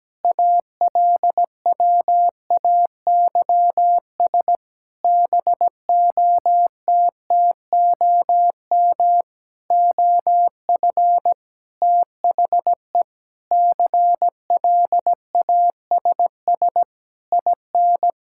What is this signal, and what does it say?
Morse code, heard as a keyed tone, about 17 wpm, 700 Hz. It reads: ALWAYS BOTTOM OF THE CLASS IN
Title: Sentences from Top 1000 Words - Review Part 6 - 17wpm